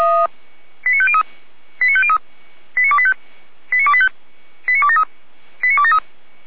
Five-tone selective call (ZVEI), heard as quick runs of five tones, tones of about 70 ms. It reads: 89503 89503 89285 89285 89273 89273